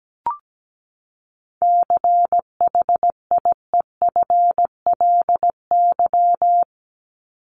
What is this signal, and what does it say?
Morse code, heard as a keyed tone, about 17 wpm, 700 Hz. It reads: CHIEFLY
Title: LY Suffix - Top 100 Words with Base Words 17wpm